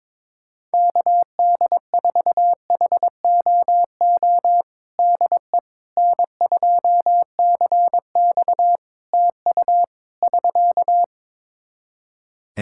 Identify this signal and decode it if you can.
Morse code, heard as a keyed tone, about 22 wpm, 700 Hz. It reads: KD4HOO DE N2CX TU <SK>
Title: Practice QSO - Exchange 4 - Ending - 22wpm